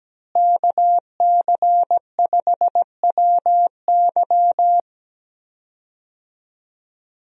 Morse code, heard as a keyed tone, about 17 wpm, 700 Hz. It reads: KC5WY
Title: USA Call Signs - 5 Characters - Repeated 3x - 17wpm